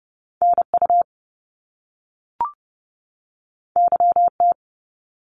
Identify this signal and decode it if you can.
Morse code, heard as a keyed tone, about 30 wpm, 700 Hz. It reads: NU YT
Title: Canadian Provinces and Territories 30wpm